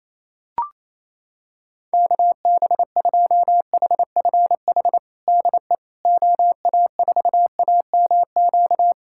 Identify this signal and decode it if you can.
Morse code, heard as a keyed tone, about 28 wpm, 700 Hz. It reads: KB2HFH DE OA4AMQ
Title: USA Callsign DE Callsign - 6 Characters - 28wpm